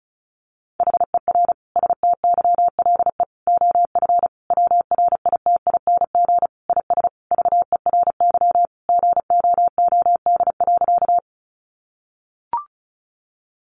Morse code, heard as a keyed tone, about 34 wpm, 700 Hz. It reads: HER STYLE OF WRITING IS VERY GOOD.
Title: Sentences from Top 800 Words 35wpm